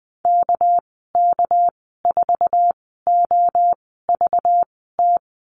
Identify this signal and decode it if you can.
Morse code, heard as a keyed tone, about 19 wpm, spaced as for 14 wpm, 700 Hz. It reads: KK4OVT
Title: 231 Lesson 37g - Letter x - Call Signs 20wpm